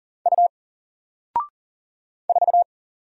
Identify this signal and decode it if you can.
Morse code, heard as a keyed tone, about 41 wpm, 700 Hz. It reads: U 4
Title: Single Character (Letter:Number:Punctuation:Prosign) Alt 40wpm